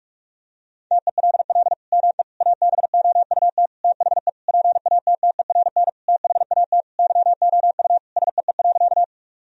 Morse code, heard as a keyed tone, about 46 wpm, 700 Hz. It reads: TELL ME ABOUT THE PATTERN THAT YOU SEE.
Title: Sentences from Top 1000 Words - Review Part 2 - 45wpm